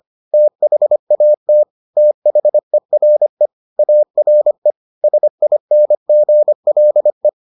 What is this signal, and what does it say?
Morse code, heard as a keyed tone, about 25 wpm, 600 Hz. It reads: THAT THERE ARE SINGLE